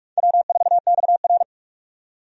Morse code, heard as a keyed tone, about 44 wpm, 700 Hz. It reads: W4XR
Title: USA Call Signs - 4 Characters - 45wpm